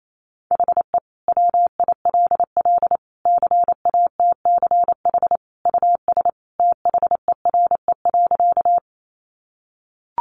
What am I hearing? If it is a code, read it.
Morse code, heard as a keyed tone, about 28 wpm, 700 Hz. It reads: HE WILL CATCH US THERE.
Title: Sentences from Top 1500 Words 28wpm